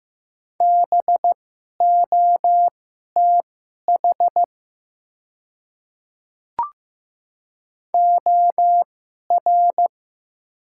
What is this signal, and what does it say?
Morse code, heard as a keyed tone, about 15 wpm, spaced as for 11 wpm, 700 Hz. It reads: BOTH OR
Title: 2x Character Spacing - Top 300 Words - 15wpm